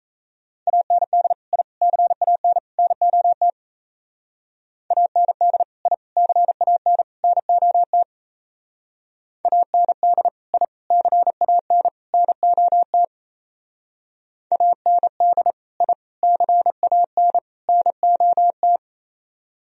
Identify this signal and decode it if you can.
Morse code, heard as a keyed tone, about 35 wpm, 700 Hz. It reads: AND I CAN NOT AND I CAN NOT AND I CAN NOT AND I CAN NOT
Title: Speed-Racing - Sets of 4 Words 28wpm